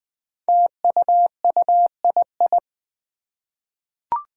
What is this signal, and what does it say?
Morse code, heard as a keyed tone, about 20 wpm, 700 Hz. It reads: TUUII
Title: Five Challenging Letter-Number Combinations 20wpm